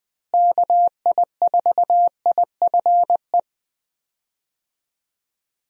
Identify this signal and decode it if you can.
Morse code, heard as a keyed tone, about 20 wpm, 700 Hz. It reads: KI4IFE
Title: USA Call Signs - 6 Characters - Repeated 3x - 20wpm